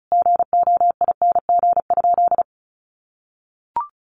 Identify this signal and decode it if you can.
Morse code, heard as a keyed tone, about 34 wpm, 700 Hz. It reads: GOING?
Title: Sentences from Top 1300 Words 35wpm